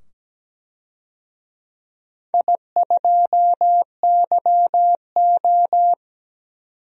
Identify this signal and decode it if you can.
Morse code, heard as a keyed tone, about 17 wpm, 700 Hz. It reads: I2YO
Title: DXCC-100 Call Signs - 4 Characters - 17wpm